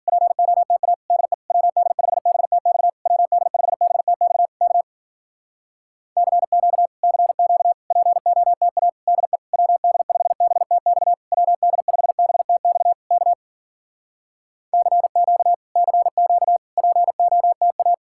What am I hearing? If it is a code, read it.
Morse code, heard as a keyed tone, about 52 wpm, 700 Hz. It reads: POTA DE WD5BTX WD5BTX K CQ CQ POTA DE WD5BTX WD5BTX K CQ CQ POTA